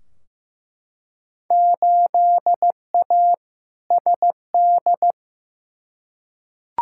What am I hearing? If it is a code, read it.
Morse code, heard as a keyed tone, about 15 wpm, 700 Hz. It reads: 8A SD
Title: Field Day Exchanges Encore - 15wpm